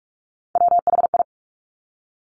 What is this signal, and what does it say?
Morse code, heard as a keyed tone, about 46 wpm, 700 Hz. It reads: WHI
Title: Top Three Letter Combinations in Words 45wpm